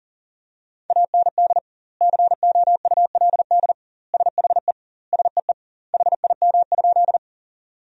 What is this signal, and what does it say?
Morse code, heard as a keyed tone, about 40 wpm, 700 Hz. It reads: AND COULD SHE SEE HIM?
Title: 2x Word-Spacing - Sentences from Top 100 Words - 40wpm